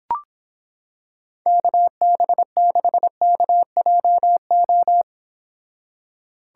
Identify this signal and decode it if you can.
Morse code, heard as a keyed tone, about 26 wpm, 700 Hz. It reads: KB6KJO